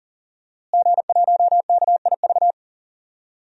Morse code, heard as a keyed tone, about 40 wpm, 700 Hz. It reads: G1KIV
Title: DXCC-100 Call Signs - 5 Characters - Repeated 2x - 40wpm